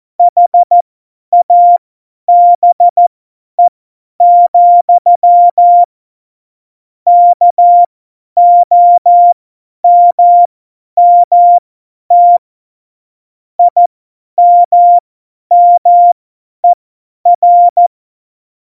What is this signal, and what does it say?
Morse code, heard as a keyed tone, about 14 wpm, spaced as for 10 wpm, 700 Hz. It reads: HABE, KOMMT IMMER
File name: Morsekurs / Witze / Witz 17.Juli 2018, 50 F70